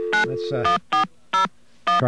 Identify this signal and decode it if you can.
DTMF, heard as telephone touch tones, tones of about 130 ms, with speech. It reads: 91401